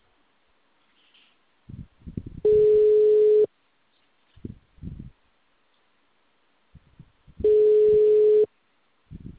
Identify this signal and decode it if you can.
Ringback tone (Europe, 425 Hz).
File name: out- -2004- - - .32413